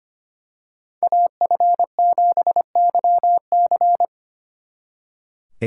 Morse code, heard as a keyed tone, about 29 wpm, 700 Hz. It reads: AF7YC
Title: Speed Racing - USA 2x2 Call Signs 25wpm